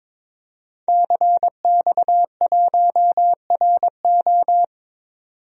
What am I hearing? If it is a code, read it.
Morse code, heard as a keyed tone, about 22 wpm, 700 Hz. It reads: CX1RO
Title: DXCC-100 Call Signs - 5 Characters - 22wpm